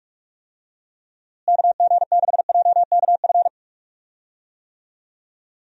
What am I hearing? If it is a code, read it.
Morse code, heard as a keyed tone, about 41 wpm, 700 Hz. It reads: KG6JKF
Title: USA Call Signs - 6 Characters - Repeated 2x - 45wpm